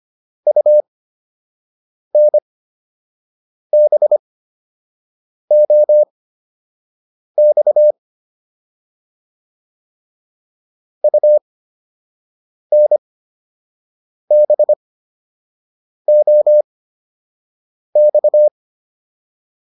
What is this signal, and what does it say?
Morse code, heard as a keyed tone, about 25 wpm, spaced as for 6 wpm, 600 Hz. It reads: UNBOX UNBOX
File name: UN5 06